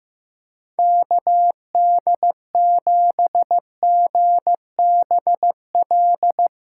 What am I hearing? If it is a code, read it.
Morse code, heard as a keyed tone, about 15 wpm, 700 Hz. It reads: KD7GBL
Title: USA Call Signs - 6 Characters - 4x Character Spacing - 15wpm